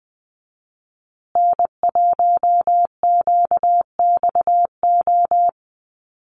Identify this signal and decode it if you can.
Morse code, heard as a keyed tone, about 20 wpm, 700 Hz. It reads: N1QXO
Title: USA Call Signs Repeated 2x 20wpm